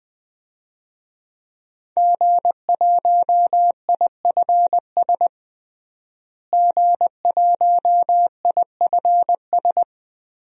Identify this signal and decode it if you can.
Morse code, heard as a keyed tone, about 19 wpm, 700 Hz. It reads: G1IFS G1IFS
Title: DXCC-100 Call Signs - 5 Characters - Repeated 3x - 20wpm